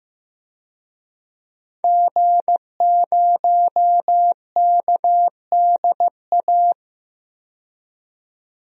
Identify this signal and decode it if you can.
Morse code, heard as a keyed tone, about 15 wpm, 700 Hz. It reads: G0KDA
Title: DXCC-100 Call Signs - 5 Characters - 15wpm